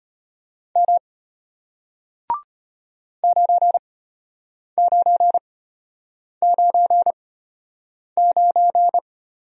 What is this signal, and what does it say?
Morse code, heard as a keyed tone, about 33 wpm, spaced as for 8 wpm, 700 Hz. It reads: M 9999